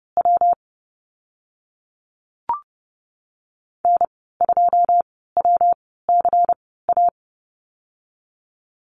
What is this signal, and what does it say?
Morse code, heard as a keyed tone, about 30 wpm, spaced as for 17 wpm, 700 Hz. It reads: W N2WCA
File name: Lesson 19g - Letter w - Call Signs 30wpm